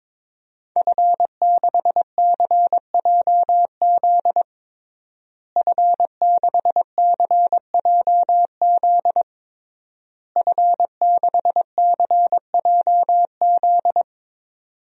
Morse code, heard as a keyed tone, about 22 wpm, 700 Hz. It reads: F6CJZ F6CJZ F6CJZ
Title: DXCC-100 Call Signs - 5 Characters - Repeated 3x - 22wpm